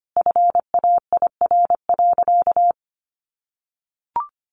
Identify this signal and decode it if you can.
Morse code, heard as a keyed tone, about 25 wpm, 700 Hz. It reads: FAIR.
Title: Sentences from Top 1300 Words 25wpm